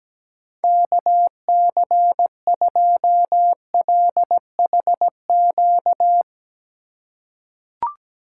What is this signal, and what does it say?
Morse code, heard as a keyed tone, about 17 wpm, 700 Hz. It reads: KC2LHQ
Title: USA Call Signs - 6 Characters - 2x Character Spacing - 17wpm